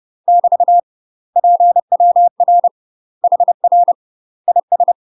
Morse code, heard as a keyed tone, about 30 wpm, 700 Hz. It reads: <BT> PWR HR IS
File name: UK MASTER QSOs TRAINING / 30WPM / UK MASTER QSOs 0014